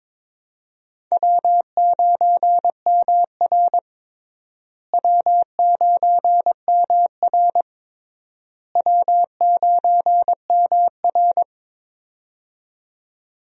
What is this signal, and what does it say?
Morse code, heard as a keyed tone, about 22 wpm, 700 Hz. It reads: W9MR W9MR W9MR
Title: USA Call Signs - 4 Characters - Repeated 3x - 22wpm